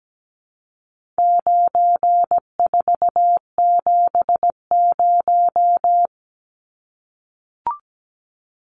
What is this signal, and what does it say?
Morse code, heard as a keyed tone, about 17 wpm, 700 Hz. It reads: 9470
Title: Numbers - 4-Digit 17wpm